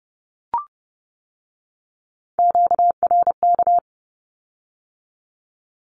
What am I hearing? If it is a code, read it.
Morse code, heard as a keyed tone, about 30 wpm, 700 Hz. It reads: QRK
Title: Q-Codes with Spoken Meaning - 30wpm